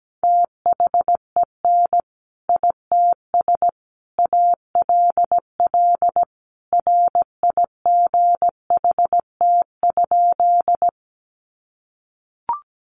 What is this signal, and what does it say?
Morse code, heard as a keyed tone, about 17 wpm, 700 Hz. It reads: THEN ITS ALL RIGHT?
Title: Sentences from Top 600 Words 17wpm